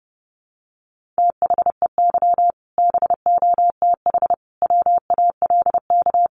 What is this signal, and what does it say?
Morse code, heard as a keyed tone, about 30 wpm, 700 Hz. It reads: THEY BOTH WALK
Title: Additional Easy Phrases 30wpm